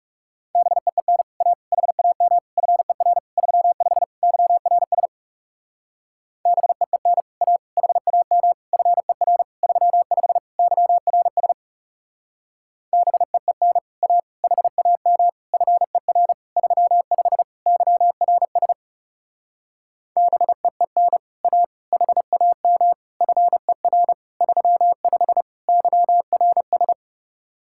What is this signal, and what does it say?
Morse code, heard as a keyed tone, about 38 wpm, 700 Hz. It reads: BEEN A HAM FER 35 YRS BEEN A HAM FER 35 YRS BEEN A HAM FER 35 YRS BEEN A HAM FER 35 YRS